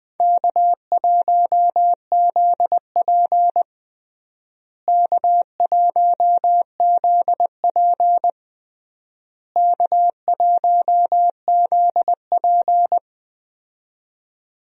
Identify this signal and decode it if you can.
Morse code, heard as a keyed tone, about 20 wpm, 700 Hz. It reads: K1ZP K1ZP K1ZP
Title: USA Call Signs - 4 Characters - Repeated 3x - 20wpm